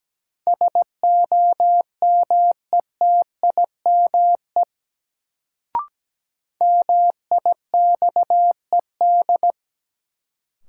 Morse code, heard as a keyed tone, about 17 wpm, 700 Hz. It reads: SOMETIME MIXED
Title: Sentences from Top 1500 Words - New Word Drill Alt - 17wpm